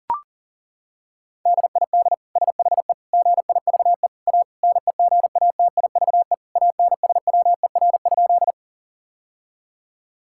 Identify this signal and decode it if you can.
Morse code, heard as a keyed tone, about 37 wpm, 700 Hz. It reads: DID SHE GIVE A NEGATIVE ANSWER?
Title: Sentences from Top 1000 Words - Review Part 6 - 40wpm